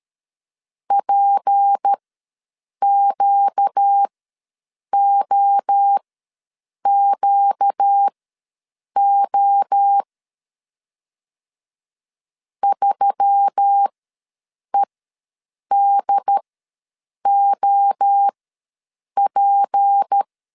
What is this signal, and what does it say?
Morse code, heard as a keyed tone, about 13 wpm, spaced as for 7 wpm, 800 Hz. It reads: PQOQO 3EDOP
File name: LEZIONI / 6 PKX0 / 6 P ws 12wpm 3117